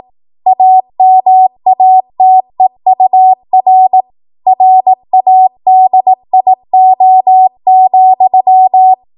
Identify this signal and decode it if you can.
Morse code, heard as a keyed tone, about 18 wpm, 750 Hz. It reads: AMATEUR RADIO,